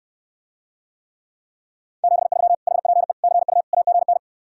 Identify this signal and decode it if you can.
Morse code, heard as a keyed tone, about 69 wpm, 700 Hz. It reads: 73 HPE CU AGN